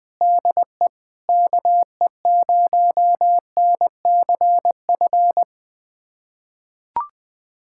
Morse code, heard as a keyed tone, about 20 wpm, 700 Hz. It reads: DE KE0NCF
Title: USA Callsign DE Callsign 20wpm